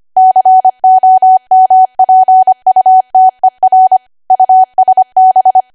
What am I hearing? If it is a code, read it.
Morse code, heard as a keyed tone, about 25 wpm, 750 Hz. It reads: COMPUTER USB
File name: cwa / W1AW-Level3 / 25WPM